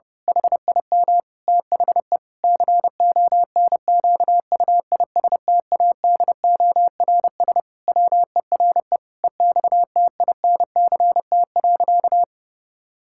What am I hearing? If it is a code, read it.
Morse code, heard as a keyed tone, about 32 wpm, 700 Hz. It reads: HIM THE CONQUISTADORS WERE EXTINCT.
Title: Pangram Challenge v2 - 30wpm